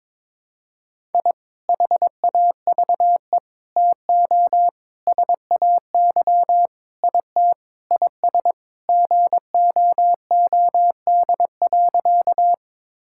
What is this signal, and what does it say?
Morse code, heard as a keyed tone, about 22 wpm, 700 Hz. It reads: I HAVE TO SAY IT IS GOOD.
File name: practice / 22wpm / Sentences with 4-Word Phrases - 22wpm